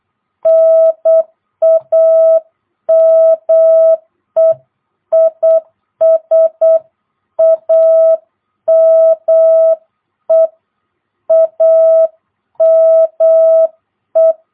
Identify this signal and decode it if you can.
Morse code, heard as a keyed tone, about 8 wpm, 660 Hz. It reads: NAME IS AME AME
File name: jh8cbh / 2022cw